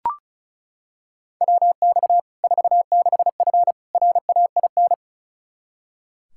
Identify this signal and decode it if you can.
Morse code, heard as a keyed tone, about 36 wpm, 700 Hz. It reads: WX 46F RAIN
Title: Practice QSO - Exchange 2 - Info - 35wpm